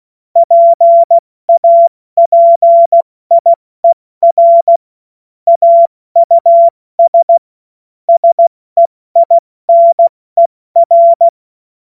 Morse code, heard as a keyed tone, about 16 wpm, 680 Hz. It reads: PAPIER AUS SEINER
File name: Morsekurs / Witze / Witz 14.November 2023 70 F80